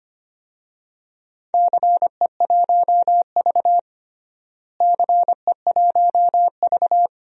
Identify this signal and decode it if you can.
Morse code, heard as a keyed tone, about 25 wpm, 700 Hz. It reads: CE1V CE1V
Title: DXCC-100 Call Signs - 4 Characters - Repeated 2x - 25wpm